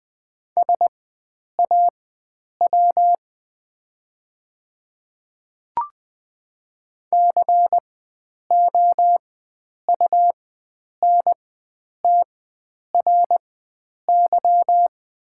Morse code, heard as a keyed tone, about 20 wpm, spaced as for 9 wpm, 700 Hz. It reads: SAW COUNTRY
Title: 4x Character Spacing - Top 300 Words - 20wpm